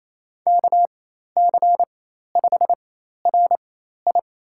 Morse code, heard as a keyed tone, about 27 wpm, spaced as for 13 wpm, 700 Hz. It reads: KC5RI
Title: USA Call Signs - 4x Character Spacing - 28wpm